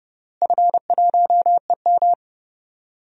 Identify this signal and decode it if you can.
Morse code, heard as a keyed tone, about 31 wpm, 700 Hz. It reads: F1EM
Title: DXCC-100 Call Signs - 4 Characters - Repeated 3x - 30wpm